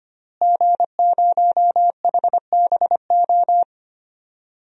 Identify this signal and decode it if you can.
Morse code, heard as a keyed tone, about 25 wpm, 700 Hz. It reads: G0HBO
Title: DXCC-100 Call Signs - 5 Characters - Repeated 3x - 25wpm